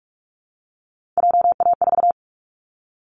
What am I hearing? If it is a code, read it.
Morse code, heard as a keyed tone, about 44 wpm, 700 Hz. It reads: JA4
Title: Callsign Prefixes for Top 50 DXCC - 45wpm